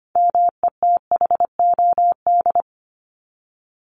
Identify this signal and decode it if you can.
Morse code, heard as a keyed tone, about 25 wpm, 700 Hz. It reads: METHOD
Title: Top 501-750 Words 25wpm